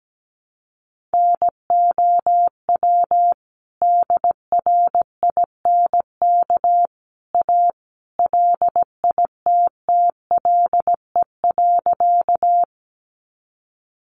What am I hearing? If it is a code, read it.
Morse code, heard as a keyed tone, about 17 wpm, 700 Hz. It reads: NOW DRINK A LITTLE.
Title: Sentences from Top 1500 Words 17wpm